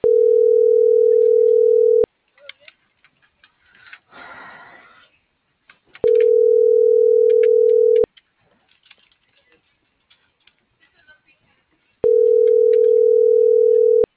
Ringback tone (North America), 440 + 480 Hz.